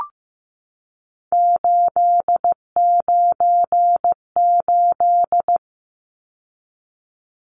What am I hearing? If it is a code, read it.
Morse code, heard as a keyed tone, about 15 wpm, 700 Hz. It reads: 898
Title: Numbers - 3-Digit 15wpm